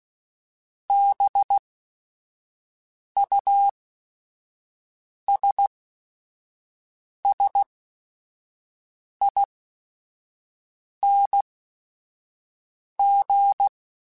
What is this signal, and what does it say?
Morse code, heard as a keyed tone, about 16 wpm, spaced as for 5 wpm, 800 Hz. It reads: BUSSING
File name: Special INGs / Special INGs-mp3s / Sngl Ing @5